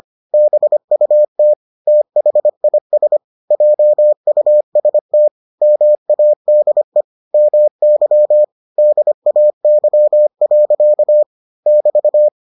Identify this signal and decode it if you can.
Morse code, heard as a keyed tone, about 25 wpm, 600 Hz. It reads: BUT THIS JUST MADE MY DAY. <BT>